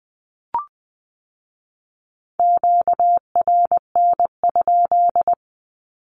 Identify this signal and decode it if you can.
Morse code, heard as a keyed tone, about 20 wpm, 700 Hz. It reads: QRN?
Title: Q-Codes with Spoken Meaning - 20wpm